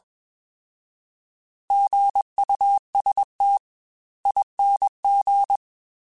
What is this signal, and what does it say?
Morse code, heard as a keyed tone, about 21 wpm, spaced as for 18 wpm, 800 Hz. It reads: GUSTING
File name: Alphabetical / Alphabetical mp3s / Gs @20